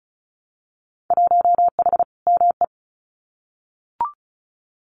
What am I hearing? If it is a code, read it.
Morse code, heard as a keyed tone, about 34 wpm, 700 Hz. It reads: 1H ME
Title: Winter Field Day Exchanges 35wpm